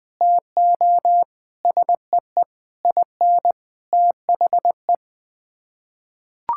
Morse code, heard as a keyed tone, about 20 wpm, 700 Hz. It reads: TO SEE IN THE
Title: Sets of 9 Words - 20wpm